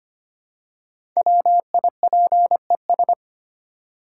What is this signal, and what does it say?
Morse code, heard as a keyed tone, about 25 wpm, 690 Hz. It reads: WIPES
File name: practice / 25wpm / Easy Words with Hints - 25wpm